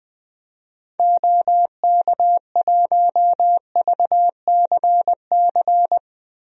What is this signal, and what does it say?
Morse code, heard as a keyed tone, about 21 wpm, 700 Hz. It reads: OK1VCC
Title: DXCC-100 Call Signs - 6 Characters - Repeated 2x - 20wpm